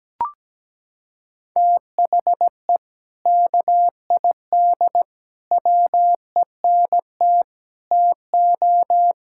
Morse code, heard as a keyed tone, about 17 wpm, 700 Hz. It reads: THE KID WENT TO
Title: Sentences from Top 1600 Words 17wpm